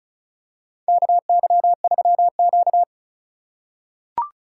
Morse code, heard as a keyed tone, about 34 wpm, 700 Hz. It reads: KY3Q
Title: USA Call Signs - 4 Characters - 2x Character Spacing - 35wpm